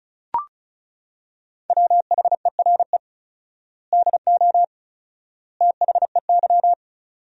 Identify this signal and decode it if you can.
Morse code, heard as a keyed tone, about 36 wpm, 700 Hz. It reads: WHERE DO THEY
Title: 4x Word-Spacing - Sentences from Top 200 Words - 35wpm